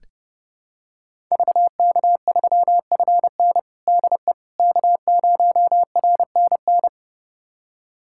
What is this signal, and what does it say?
Morse code, heard as a keyed tone, about 30 wpm, 700 Hz. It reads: VK3FN DE K0RNN
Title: USA Callsign DE Callsign - 5 Characters - 30wpm